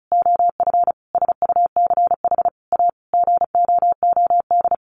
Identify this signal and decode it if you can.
Morse code, heard as a keyed tone, about 34 wpm, 700 Hz. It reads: OF SUCH A GOOD
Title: Sentences from Top 1500 Words 35wpm